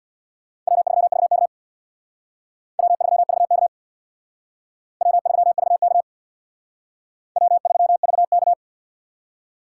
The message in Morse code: W3VK W3VK W3VK W3VK

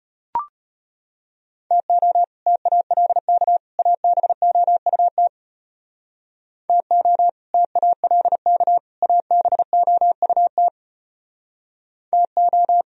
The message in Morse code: TO TALK ABOUT TO TALK ABOUT TO

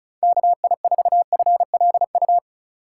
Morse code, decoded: KI4FLU

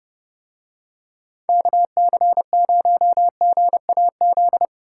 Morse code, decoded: KC0GAZ

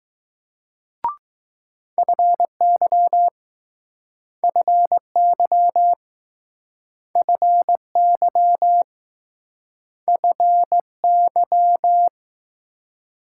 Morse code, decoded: FY FY FY FY

